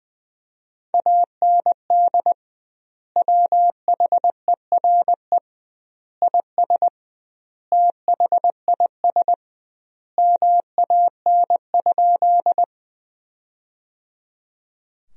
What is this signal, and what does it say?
Morse code, heard as a keyed tone, about 20 wpm, 700 Hz. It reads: AND WHERE IS THIS MAN?
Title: 2x Word-Spacing - Sentences from Top 200 Words - 20wpm